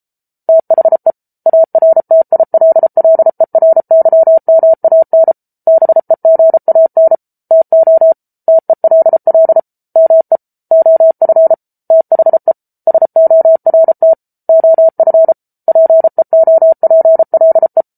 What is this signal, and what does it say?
Morse code, heard as a keyed tone, about 34 wpm, 650 Hz. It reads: THE ARTILLERYMAN BEGAN TO TELL ME OF THE SORT OF PEOPLE